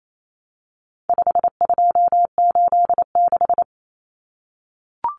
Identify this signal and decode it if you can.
Morse code, heard as a keyed tone, about 28 wpm, 700 Hz. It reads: 5286
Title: Numbers - 4-Digit 28wpm